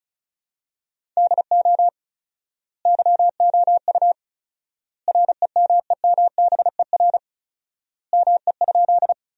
Morse code, heard as a keyed tone, about 36 wpm, 700 Hz. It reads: DO YOU REMEMBER ME?
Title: 4x Word-Spacing - Sentences from Top 400 Words - 35wpm